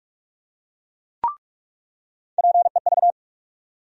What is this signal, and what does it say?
Morse code, heard as a keyed tone, about 46 wpm, 700 Hz. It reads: PEV
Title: USA Call Sign Postfixes - 3 Letters 45wpm